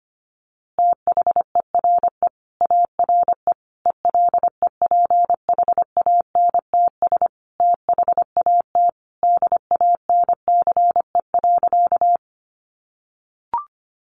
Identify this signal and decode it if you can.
Morse code, heard as a keyed tone, about 25 wpm, 700 Hz. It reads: THERE ARE ELEPHANTS THAT DANCE.
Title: Easy Phrases 25wpm